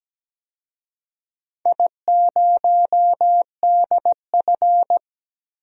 Morse code, heard as a keyed tone, about 17 wpm, 700 Hz. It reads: I0DF
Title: DXCC-100 Call Signs - 4 Characters - 17wpm